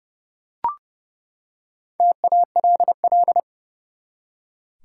Morse code, heard as a keyed tone, about 30 wpm, 700 Hz. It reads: TALL
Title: Four Letter Words - 30wpm